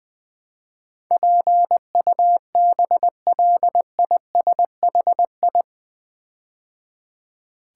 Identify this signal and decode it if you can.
Morse code, heard as a keyed tone, about 20 wpm, 700 Hz. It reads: PUBLISHI
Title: Build A Word - 4 Letters At A Time - 10-Letter Words - 20wpm